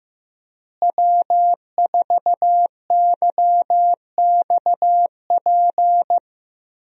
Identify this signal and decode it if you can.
Morse code, heard as a keyed tone, about 15 wpm, 700 Hz. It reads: W4YXP
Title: USA Call Signs - 5 Characters - Repeated 3x - 15wpm